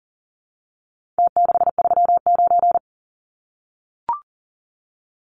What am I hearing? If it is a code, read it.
Morse code, heard as a keyed tone, about 39 wpm, 700 Hz. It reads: T639
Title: Numbers - Contest Style - 4-Digit 40wpm